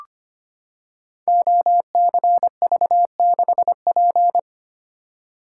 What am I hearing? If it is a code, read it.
Morse code, heard as a keyed tone, about 25 wpm, 700 Hz. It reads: OCV6P
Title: Code Groups - 5 Letter:Numbers - 25wpm